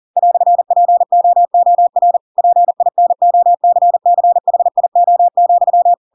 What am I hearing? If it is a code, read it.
Morse code, heard as a keyed tone, about 40 wpm, 700 Hz. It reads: "POOR PINOCCHIO,
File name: Pinocchio / Pinocchio ch4 40 40 ik1hge